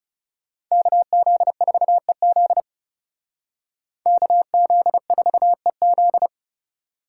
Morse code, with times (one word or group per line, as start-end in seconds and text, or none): 0.7-2.6 KZ4EZ
4.1-6.3 KZ4EZ